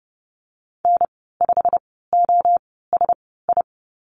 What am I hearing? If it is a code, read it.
Morse code, heard as a keyed tone, about 32 wpm, spaced as for 17 wpm, 700 Hz. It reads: N5OSI
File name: Lesson 15g - Number 5 - Call Signs 30wpm